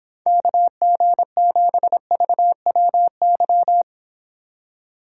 Morse code, heard as a keyed tone, about 26 wpm, 700 Hz. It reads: KG7VWY